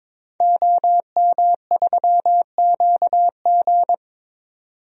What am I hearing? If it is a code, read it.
Morse code, heard as a keyed tone, about 22 wpm, 700 Hz. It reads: OM3QG
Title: DXCC-100 Call Signs - 5 Characters - 22wpm